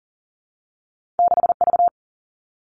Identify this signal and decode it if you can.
Morse code, heard as a keyed tone, about 42 wpm, 700 Hz. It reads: 6 V
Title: Two Challenging Letter-Number Combinations 40wpm